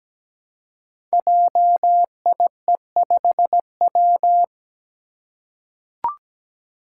Morse code, heard as a keyed tone, about 17 wpm, 700 Hz. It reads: JIE5W
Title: Code Groups - 5 Letter:Numbers - 17wpm